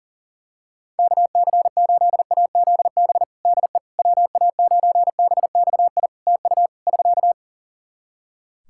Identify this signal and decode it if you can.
Morse code, heard as a keyed tone, about 37 wpm, 700 Hz. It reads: KC8AZB DE WA9BXI TU <SK>
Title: K1USN SST QSO Practice 40wpm